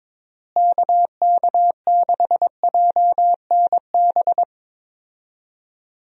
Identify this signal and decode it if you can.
Morse code, heard as a keyed tone, about 22 wpm, 700 Hz. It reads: KK6JNB
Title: USA Call Signs - 6 Characters - Repeated 3x - 22wpm